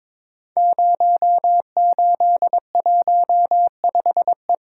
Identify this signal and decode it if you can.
Morse code, heard as a keyed tone, about 23 wpm, 700 Hz. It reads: 0815E
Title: Code Groups - 5 Letter:Numbers - 22wpm